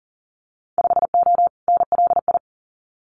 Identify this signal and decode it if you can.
Morse code, heard as a keyed tone, about 40 wpm, 700 Hz. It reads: 5O NLI
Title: Winter Field Day Exchanges 40wpm